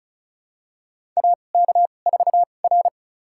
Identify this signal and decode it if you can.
Morse code, heard as a keyed tone, about 34 wpm, spaced as for 25 wpm, 700 Hz. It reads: AK4R
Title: USA Call Signs - 4 Characters - 2x Character Spacing - 35wpm